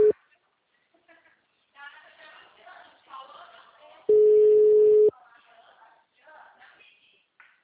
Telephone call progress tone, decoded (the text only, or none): ringback tone